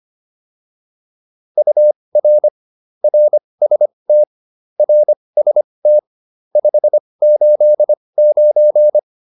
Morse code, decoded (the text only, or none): UR RST RST 589